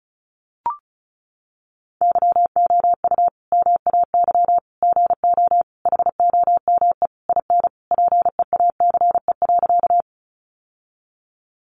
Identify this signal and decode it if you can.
Morse code, heard as a keyed tone, about 34 wpm, 700 Hz. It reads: YOU MAY GO HOME IN PEACE.
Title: Sentences from Top 1500 Words 35wpm